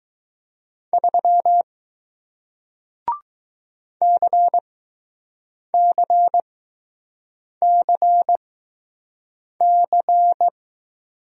Morse code, heard as a keyed tone, about 20 wpm, spaced as for 6 wpm, 700 Hz. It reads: 3 CCCC